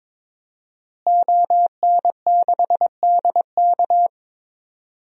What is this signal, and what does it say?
Morse code, heard as a keyed tone, about 22 wpm, 700 Hz. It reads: ON6DK
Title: DXCC-100 Call Signs - 5 Characters - Repeated 3x - 22wpm